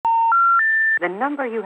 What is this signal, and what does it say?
Special information tone, with speech.